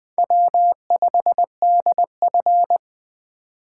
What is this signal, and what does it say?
Morse code, heard as a keyed tone, about 20 wpm, 700 Hz. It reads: W5DF
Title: USA Call Signs - 4 Characters - 2x Character Spacing - 20wpm